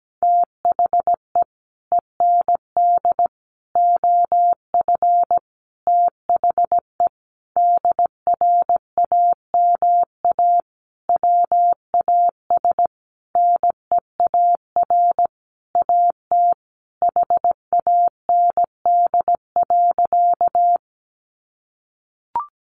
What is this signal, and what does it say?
Morse code, heard as a keyed tone, about 17 wpm, 700 Hz. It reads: THE END OF THE DRAMA WAS NEAR AT HAND.
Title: Sentences from Top 1500 Words 17wpm